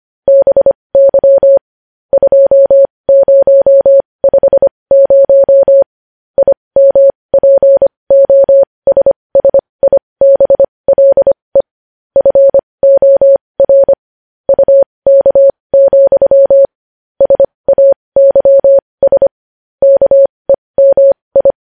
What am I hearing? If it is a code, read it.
Morse code, heard as a keyed tone, about 25 wpm, spaced as for 20 wpm, 550 Hz. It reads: BY 2050 IMPOSSIBLE FOR UK, SAYS KEMI
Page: news-headlines-in-morse-code-at-20-wpm